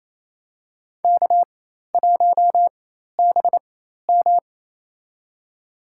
Morse code, heard as a keyed tone, about 28 wpm, spaced as for 13 wpm, 700 Hz. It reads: K1BM
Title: USA Call Signs - 4x Character Spacing - 28wpm